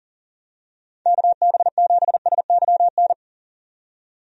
Morse code, heard as a keyed tone, about 38 wpm, 700 Hz. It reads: KB7SYN
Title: USA Call Signs - 6 Characters - 40wpm